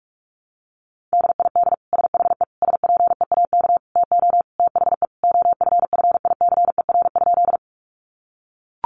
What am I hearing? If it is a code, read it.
Morse code, heard as a keyed tone, about 44 wpm, 700 Hz. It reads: DID SHE SPEAK TO THE OFFICER?
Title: Sentences from Top 1200 Words 45wpm